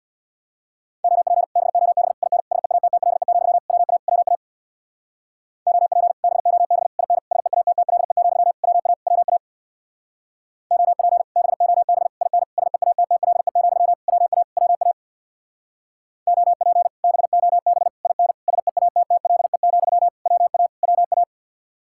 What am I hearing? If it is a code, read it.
Morse code, heard as a keyed tone, about 62 wpm, 700 Hz. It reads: OP BOB IN SEATTLE, WA WA OP BOB IN SEATTLE, WA WA OP BOB IN SEATTLE, WA WA OP BOB IN SEATTLE, WA WA
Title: Speed-Racing Common QSO Phrases - 50wpm